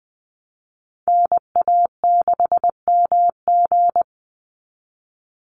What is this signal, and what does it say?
Morse code, heard as a keyed tone, about 20 wpm, 700 Hz. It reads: NA6MG
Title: NA SOTA Activator Callsigns 20wpm